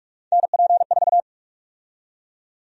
Morse code, heard as a keyed tone, about 46 wpm, 700 Hz. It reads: NP4
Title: USA Call Sign Prefixes - 45wpm